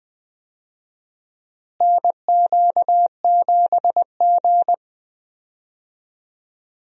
Morse code, heard as a keyed tone, about 20 wpm, 700 Hz. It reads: NQ7G
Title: USA Call Signs - 4 Characters - Repeated 3x - 20wpm